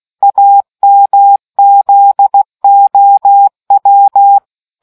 Morse code, heard as a keyed tone, about 16 wpm, 800 Hz. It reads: AMZOW